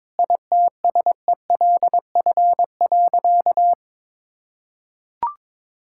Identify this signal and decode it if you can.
Morse code, heard as a keyed tone, about 22 wpm, 700 Hz. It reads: ITSELF.
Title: Sentences from Top 1000 Words - Review Part 5 - 22wpm